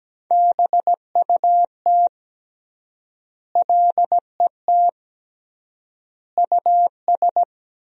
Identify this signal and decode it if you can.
Morse code, heard as a keyed tone, about 17 wpm, 700 Hz. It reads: BUT LET US
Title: 3x Word-Spacing - Sentences from Top 500 Words - 17wpm